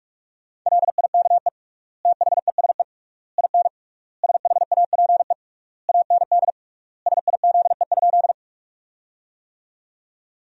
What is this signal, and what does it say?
Morse code, heard as a keyed tone, about 41 wpm, 700 Hz. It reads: LIKE THESE IN SHAPE AND SIZE?
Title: 3x Word-Spacing - Sentences from Top 500 Words - 45wpm